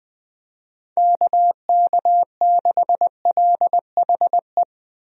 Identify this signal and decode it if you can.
Morse code, heard as a keyed tone, about 20 wpm, 700 Hz. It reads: KK6LHE
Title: USA Call Signs - 6 Characters - Repeated 2x - 20wpm